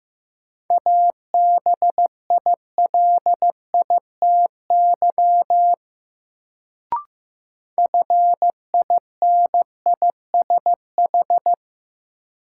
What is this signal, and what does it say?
Morse code, heard as a keyed tone, about 15 wpm, 700 Hz. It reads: ABILITY FINISH
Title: Sentences from Top 1200 Words - New Word Drill Alt - 15wpm